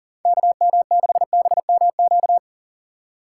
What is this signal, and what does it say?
Morse code, heard as a keyed tone, about 40 wpm, 700 Hz. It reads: KM6BMQ
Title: USA Call Signs - 6 Characters - 3x Character Spacing - 40wpm